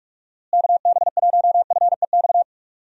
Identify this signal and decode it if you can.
Morse code, heard as a keyed tone, about 46 wpm, 700 Hz. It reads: KB1FEX
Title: USA Call Signs - 6 Characters - Repeated 2x - 45wpm